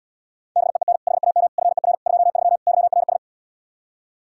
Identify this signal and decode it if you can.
Morse code, heard as a keyed tone, about 77 wpm, 700 Hz. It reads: BEEN HAM FER 23 YRS